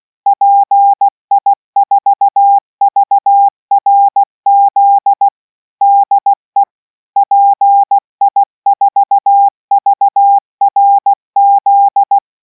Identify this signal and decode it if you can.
Morse code, heard as a keyed tone, about 16 wpm, 800 Hz. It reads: PI4VRZ DE PI4VRZ